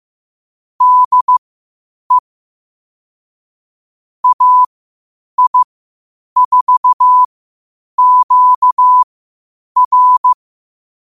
Morse code, heard as a keyed tone, about 15 wpm, spaced as for 8 wpm, 1000 Hz. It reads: DE AI4QR
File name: qotdfiles / 8wpm